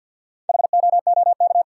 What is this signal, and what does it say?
Morse code, heard as a keyed tone, about 49 wpm, 700 Hz. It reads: HOOK
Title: Sentences from Top 1600 Words - New Word Drill - 50wpm